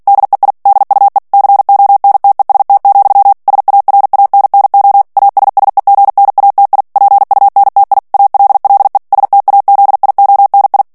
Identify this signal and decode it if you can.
Morse code, heard as a keyed tone, about 48 wpm, 800 Hz. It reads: DEI DUE CONTEST, SARANNO ASSEGNATI PUNTI ALLE STAZIONI
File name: ITALIEN texte clair 2 vitesse 45 mots minute